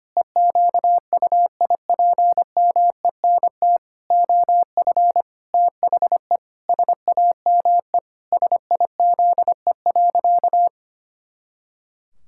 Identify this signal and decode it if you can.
Morse code, heard as a keyed tone, about 25 wpm, 700 Hz. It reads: EQUIPMENT OF THE SAME SIZE.
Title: Pangram Challenge v2 - 25wpm